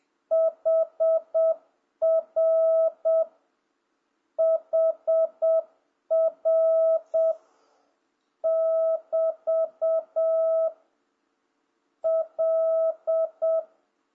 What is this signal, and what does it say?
Morse code, heard as a keyed tone, about 7 wpm, spaced as for 5 wpm, 650 Hz. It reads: HRHR<BT>L